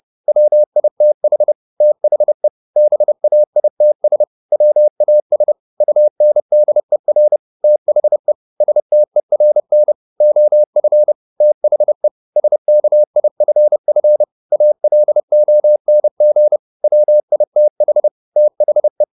Morse code, heard as a keyed tone, about 30 wpm, 600 Hz. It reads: WITH THE BAITS WAS UNDER THE STERN OF THE SKIFF ALONG WITH THE
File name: the old man and the sea / the old man and the sea 1 30wpm